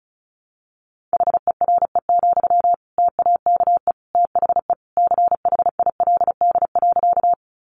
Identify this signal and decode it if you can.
Morse code, heard as a keyed tone, about 34 wpm, 700 Hz. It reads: HERE, TAKE THE CHILD.
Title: Sentences from Top 800 Words 35wpm